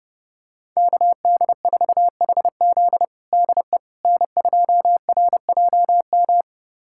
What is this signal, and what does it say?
Morse code, heard as a keyed tone, about 32 wpm, 700 Hz. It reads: KD4HZ DE N2RJM
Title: USA Callsign DE Callsign - 5 Characters - 30wpm